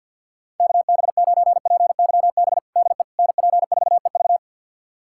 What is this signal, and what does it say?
Morse code, heard as a keyed tone, about 52 wpm, 700 Hz. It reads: KB9PYB DE NP4EV
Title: USA Callsign DE Callsign 50wpm